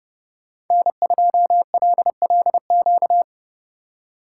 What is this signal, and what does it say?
Morse code, heard as a keyed tone, about 29 wpm, 700 Hz. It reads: N2LLQ